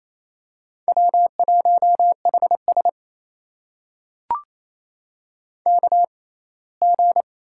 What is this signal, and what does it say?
Morse code, heard as a keyed tone, about 28 wpm, spaced as for 19 wpm, 700 Hz. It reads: W1HS KG
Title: USA Call Signs - 6x Character Spacing - 28wpm